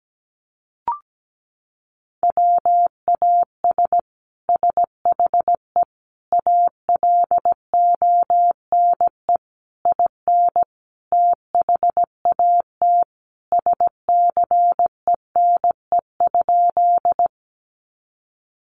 Morse code, heard as a keyed tone, about 17 wpm, 700 Hz. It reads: WAS SHE ALONE IN THAT SCENE?
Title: Sentences from Top 1100 Words 17wpm